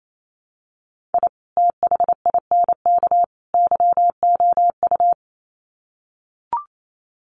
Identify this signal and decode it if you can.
Morse code, heard as a keyed tone, about 28 wpm, 700 Hz. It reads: I THINK YOU
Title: Sets of 3 Words 28wpm